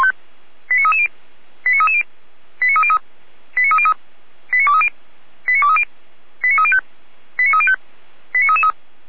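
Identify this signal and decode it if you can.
Five-tone selective call (ZVEI), heard as quick runs of five tones, tones of about 70 ms. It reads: 89330 89330 89303 89303 89230 89230 89305 89305 89333
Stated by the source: ZVEI five-tone selective call received by SDR, from feuerwehr-orlamuende.org fire brigade audio archive.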